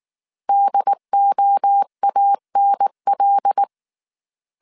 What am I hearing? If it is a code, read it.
Morse code, heard as a keyed tone, about 19 wpm, 800 Hz. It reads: DOANL